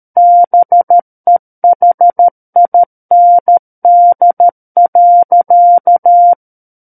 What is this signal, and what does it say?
Morse code, heard as a keyed tone, about 13 wpm, 700 Hz. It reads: BEHIND.
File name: Book One / 04-Chapter 4 - 13 WPM